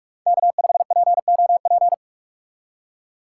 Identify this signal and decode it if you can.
Morse code, heard as a keyed tone, about 45 wpm, 700 Hz. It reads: K5POP
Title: USA Call Signs - 5 Characters - 3x Character Spacing - 45wpm